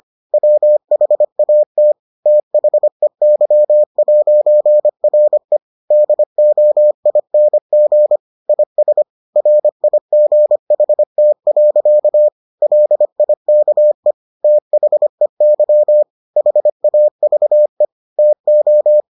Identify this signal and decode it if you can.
Morse code, heard as a keyed tone, about 25 wpm, 600 Hz. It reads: WHAT THEY'RE DOING IS RIGHT. LIKE THEY HAVE TO